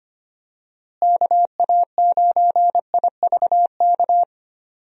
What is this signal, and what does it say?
Morse code, heard as a keyed tone, about 25 wpm, 700 Hz. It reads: KA9IVK
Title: USA Call Signs - 6 Characters - Repeated 2x - 25wpm